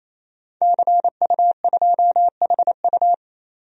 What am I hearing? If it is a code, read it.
Morse code, heard as a keyed tone, about 27 wpm, 700 Hz. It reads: CU2HU
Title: DXCC-100 Call Signs - 5 Characters - 28wpm